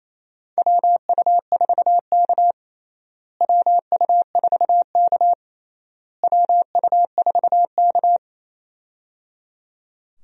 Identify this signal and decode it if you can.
Morse code, heard as a keyed tone, about 28 wpm, 700 Hz. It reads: WU4K WU4K WU4K
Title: USA Call Signs - 4 Characters - Repeated 3x - 28wpm